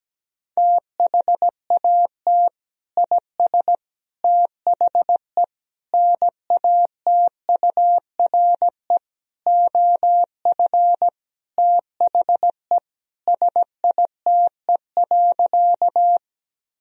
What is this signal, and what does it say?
Morse code, heard as a keyed tone, about 17 wpm, 700 Hz. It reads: THAT IS THE NATURE OF THE SITE.
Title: Sentences from Top 1000 Words - Review Part 4 - 17wpm